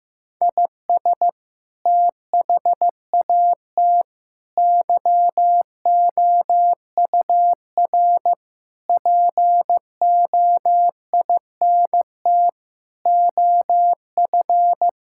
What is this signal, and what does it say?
Morse code, heard as a keyed tone, about 15 wpm, 700 Hz. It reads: IS THAT YOUR POINT OF
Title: Sentences from Top 1000 Words - Review Part 6 - 15wpm